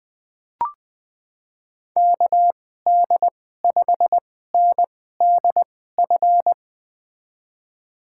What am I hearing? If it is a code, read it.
Morse code, heard as a keyed tone, about 20 wpm, spaced as for 15 wpm, 700 Hz. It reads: KD5NDF
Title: USA Call Signs - 6 Characters - 2x Character Spacing - 20wpm